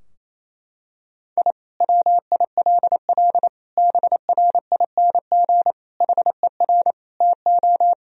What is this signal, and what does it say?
Morse code, heard as a keyed tone, about 28 wpm, 700 Hz. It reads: I WILL BRING HER TO
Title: Sentences from Top 1000 Words - Review Part 3 - 28wpm